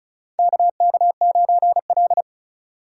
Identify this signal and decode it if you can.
Morse code, heard as a keyed tone, about 36 wpm, 700 Hz. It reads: KK9L
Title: USA Call Signs - 2x Character Spacing - 35wpm